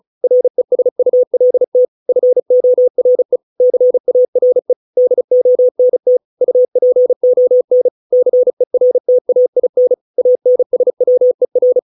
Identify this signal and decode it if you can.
Morse code, heard as a keyed tone, about 34 wpm, 500 Hz. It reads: RESULT FORE CARE DONT UPON CERTAIN ANSWER